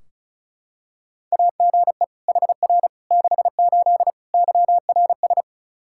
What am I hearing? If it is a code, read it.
Morse code, heard as a keyed tone, about 34 wpm, 700 Hz. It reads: AGE HR 68 YRS
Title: K1USN SST QSO Practice 35wpm